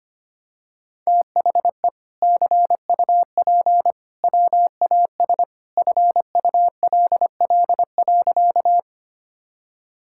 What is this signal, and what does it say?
Morse code, heard as a keyed tone, about 25 wpm, 700 Hz. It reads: THE CUP WAS FULL.
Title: Sentences from Top 1600 Words 25wpm